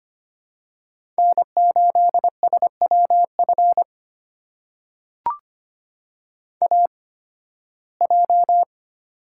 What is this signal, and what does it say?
Morse code, heard as a keyed tone, about 25 wpm, spaced as for 16 wpm, 700 Hz. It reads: N8SWF AJ
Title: USA Call Signs - 5 Characters - 8x Character Spacing - 25wpm